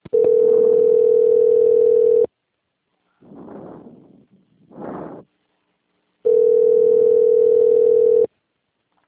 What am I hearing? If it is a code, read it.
Ringback tone (North America), 440 + 480 Hz.
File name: out- - - - - .713